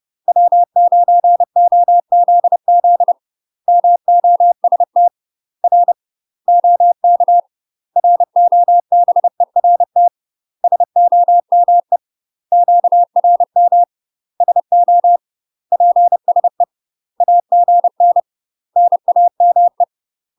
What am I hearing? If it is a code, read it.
Morse code, heard as a keyed tone, about 30 wpm, 700 Hz. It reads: W9OZZ MOST R OK ROBERT SOME QRM SO PSE AGN NAME
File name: UK MASTER QSOs TRAINING / 30WPM / UK MASTER QSOs 0046